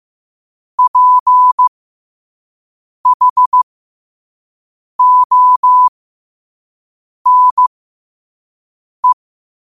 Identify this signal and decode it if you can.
Morse code, heard as a keyed tone, about 15 wpm, spaced as for 5 wpm, 1000 Hz. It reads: PHONE
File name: qotdfiles / 5wpm